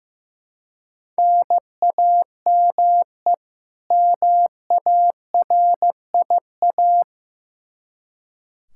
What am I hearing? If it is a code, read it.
Morse code, heard as a keyed tone, about 15 wpm, 700 Hz. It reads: NAME MARIA
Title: Practice QSO - Exchange 1 - Beginning - 15wpm